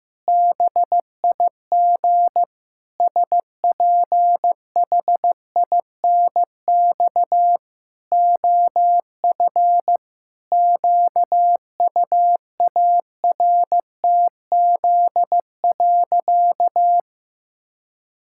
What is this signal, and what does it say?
Morse code, heard as a keyed tone, about 15 wpm, 700 Hz. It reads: BIG SPHINX OF QUARTZ.
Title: Pangram Challenge v2 - 15wpm